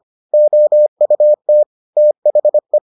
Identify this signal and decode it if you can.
Morse code, heard as a keyed tone, about 25 wpm, 600 Hz. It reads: OUT THE